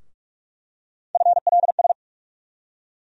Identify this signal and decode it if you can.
Morse code, heard as a keyed tone, about 42 wpm, 700 Hz. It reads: FLS
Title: USA Call Sign Postfixes - 3 Letters 45wpm